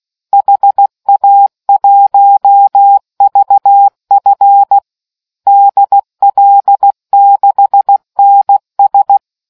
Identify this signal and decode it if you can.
Morse code, heard as a keyed tone, about 16 wpm, 800 Hz. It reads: HA1VF DL6NS